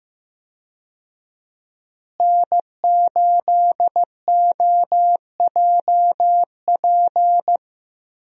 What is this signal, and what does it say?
Morse code, heard as a keyed tone, about 15 wpm, 700 Hz. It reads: N8OJP
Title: USA Call Signs - 5 Characters - Repeated 3x - 15wpm